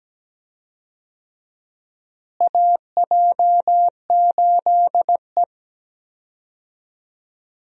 Morse code, heard as a keyed tone, about 17 wpm, 700 Hz. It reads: AJ8E
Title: USA Call Signs - 4 Characters - Repeated 2x - 17wpm